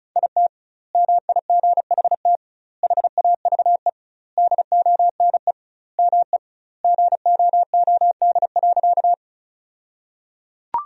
Morse code, heard as a keyed tone, about 36 wpm, 700 Hz. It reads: IT MIGHT HAVE DONE ME GOOD.
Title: 2x Word-Spacing - Sentences from Top 500 Words - 35wpm